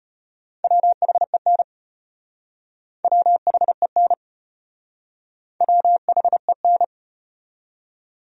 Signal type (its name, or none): Morse code